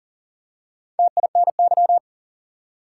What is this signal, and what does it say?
Morse code, heard as a keyed tone, about 38 wpm, 700 Hz. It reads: TINY
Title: Four Letter Words - 40wpm